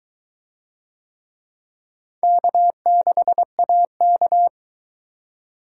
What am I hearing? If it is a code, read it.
Morse code, heard as a keyed tone, about 21 wpm, 700 Hz. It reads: K6AK